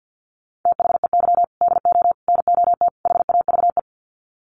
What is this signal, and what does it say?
Morse code, heard as a keyed tone, about 50 wpm, 700 Hz. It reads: THEY DO NOT HAVE
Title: Sets of 4 Words 50wpm